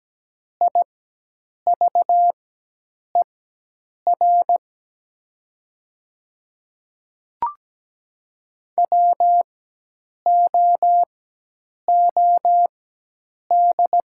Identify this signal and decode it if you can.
Morse code, heard as a keyed tone, about 17 wpm, spaced as for 8 wpm, 700 Hz. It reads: IVER WOOD
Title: 4x Character Spacing - Top 300 Words - 17wpm